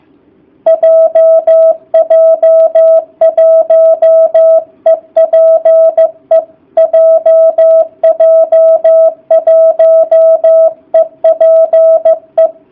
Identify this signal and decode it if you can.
Morse code, heard as a keyed tone, about 15 wpm, 660 Hz. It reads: JJ1EPE JJ1EPE